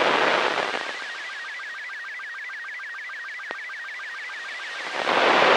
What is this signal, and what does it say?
SSTV (slow-scan television), Robot 72.
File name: arissat-aug05-2011